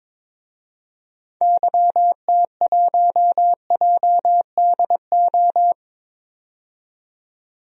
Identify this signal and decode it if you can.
Morse code, heard as a keyed tone, about 22 wpm, 700 Hz. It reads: YT1JDO
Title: DXCC-100 Call Signs - 6 Characters - 22wpm